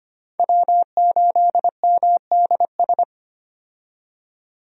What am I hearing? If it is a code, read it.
Morse code, heard as a keyed tone, about 25 wpm, 700 Hz. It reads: W8MDS
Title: USA Call Signs - 5 Characters - 25wpm